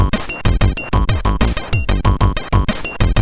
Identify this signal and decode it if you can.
There is no code.